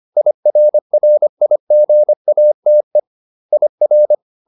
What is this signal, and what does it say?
Morse code, heard as a keyed tone, about 25 wpm, 600 Hz. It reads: IRRIGATE IR